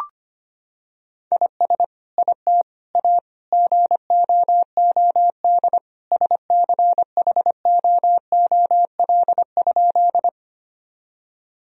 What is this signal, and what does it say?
Morse code, heard as a keyed tone, about 25 wpm, 700 Hz. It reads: IS IT A GOOD SCHOOL?
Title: Sentences from Top 1000 Words - Review Part 1 - 25wpm